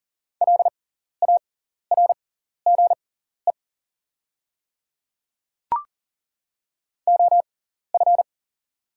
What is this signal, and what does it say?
Morse code, heard as a keyed tone, about 40 wpm, spaced as for 14 wpm, 690 Hz. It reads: LARGE OF
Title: 6x Character Spacing - Top 200 Words - 40wpm